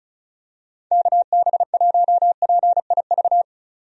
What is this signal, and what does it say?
Morse code, heard as a keyed tone, about 36 wpm, 700 Hz. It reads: KB1PIV
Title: USA Call Signs - 6 Characters - 35wpm